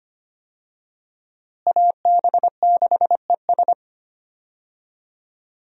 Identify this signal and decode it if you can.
Morse code, heard as a keyed tone, about 25 wpm, 700 Hz. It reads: AB6ES
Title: USA Call Signs - 5 Characters - Repeated 2x - 25wpm